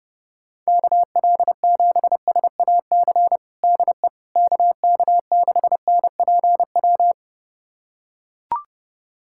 Morse code, heard as a keyed tone, about 30 wpm, 700 Hz. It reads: KL7SAC DE KK6NPW
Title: USA Callsign DE Callsign - 6 Characters - 30wpm